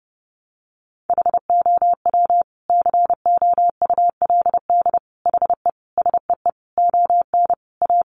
Morse code, heard as a keyed tone, about 30 wpm, 700 Hz. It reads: HOW COULD HE SEE ON A
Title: Sentences from Top 1300 Words 30wpm